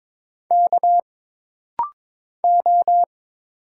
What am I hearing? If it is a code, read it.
Morse code, heard as a keyed tone, about 23 wpm, 700 Hz. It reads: K O
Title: Single Letters - Rapid-Fire 22wpm